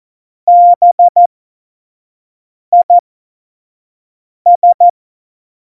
Morse code, heard as a keyed tone, about 14 wpm, spaced as for 5 wpm, 700 Hz. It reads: BIS
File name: Morsekurs / Witze / Witz 13 March 2018, 25 F70